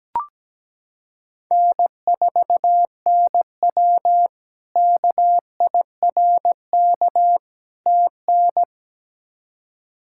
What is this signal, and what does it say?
Morse code, heard as a keyed tone, about 17 wpm, 700 Hz. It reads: N4NW KIRK TN
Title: K1USN SST QSO Practice - 17wpm